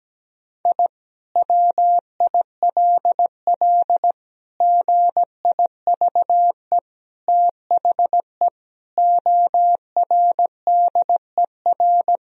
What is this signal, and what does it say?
Morse code, heard as a keyed tone, about 17 wpm, 700 Hz. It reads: I WILL GIVE THE ORDER
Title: Sentences from Top 1000 Words - Review Part 2 - 17wpm